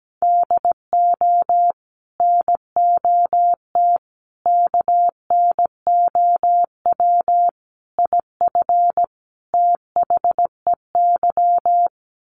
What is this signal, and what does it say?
Morse code, heard as a keyed tone, about 17 wpm, 700 Hz. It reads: DO NOT KNOW IF THEY
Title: Sets of 5 Words Encore 17wpm